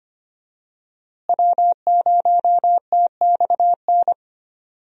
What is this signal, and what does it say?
Morse code, heard as a keyed tone, about 25 wpm, 700 Hz. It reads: W0TXN
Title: USA Call Signs - 5 Characters - Repeated 3x - 25wpm